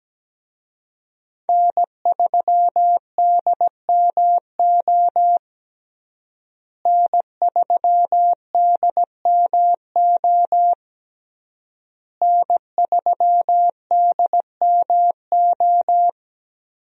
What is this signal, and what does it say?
Morse code, heard as a keyed tone, about 17 wpm, 700 Hz. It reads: N3DMO N3DMO N3DMO
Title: USA Call Signs - 5 Characters - Repeated 3x - 17wpm